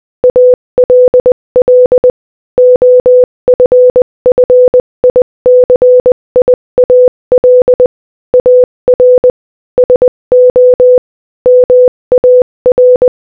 Morse code, heard as a keyed tone, about 20 wpm, 500 Hz. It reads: ALL OFFICIAL AR SO MAR
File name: Morse-Code-Audio-Speed-20-11